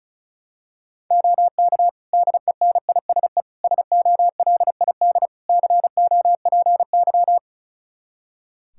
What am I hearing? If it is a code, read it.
Morse code, heard as a keyed tone, about 36 wpm, 700 Hz. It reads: OK DENISE SOLID COPY
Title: K1USN SST QSO Practice 35wpm